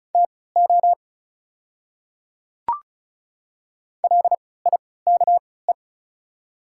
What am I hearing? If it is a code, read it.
Morse code, heard as a keyed tone, about 36 wpm, spaced as for 20 wpm, 700 Hz. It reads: TO LIKE
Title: 3x Character Spacing - Top 200 Words - 35wpm